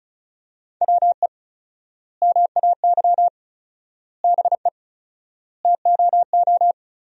Morse code, heard as a keyed tone, about 36 wpm, 700 Hz. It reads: WE MAY BE TOO
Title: 4x Word-Spacing - Sentences from Top 300 Words - 35wpm